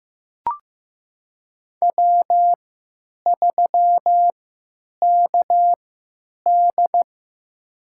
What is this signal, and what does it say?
Morse code, heard as a keyed tone, about 16 wpm, spaced as for 9 wpm, 700 Hz. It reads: W3KD
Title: USA Call Signs - 4 Characters - 3x Character Spacing - 15wpm